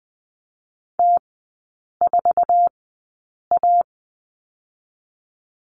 Morse code, heard as a keyed tone, about 20 wpm, 700 Hz. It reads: T 4 A
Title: 047 Lesson 9e - Number 4 - Three Characters 20wpm